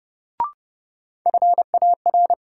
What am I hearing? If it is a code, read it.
Morse code, heard as a keyed tone, about 32 wpm, 700 Hz. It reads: FAR
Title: Sentences from Top 300 Words - New Word Drill - 30wpm